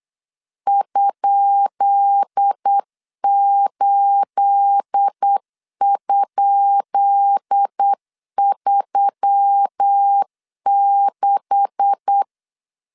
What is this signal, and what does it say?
Morse code, heard as a keyed tone, about 8 wpm, 800 Hz. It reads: ?8?36
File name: LEZIONI / 10 SIGNS / 10B INT 8wpm 3534